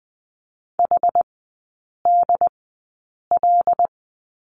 Morse code, heard as a keyed tone, about 20 wpm, spaced as for 8 wpm, 700 Hz. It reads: HDL
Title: 074 Lesson 13e - Letter L - Three Characters 20wpm